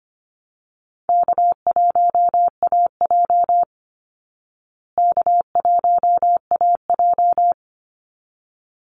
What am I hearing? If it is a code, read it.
Morse code, heard as a keyed tone, about 25 wpm, 700 Hz. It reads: K1AJ K1AJ
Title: Easy Call Signs Repeated 2x - 25wpm